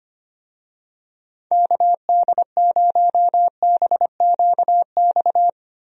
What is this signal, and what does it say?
Morse code, heard as a keyed tone, about 25 wpm, 700 Hz. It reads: KD0BQX
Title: USA Call Signs - 6 Characters - 25wpm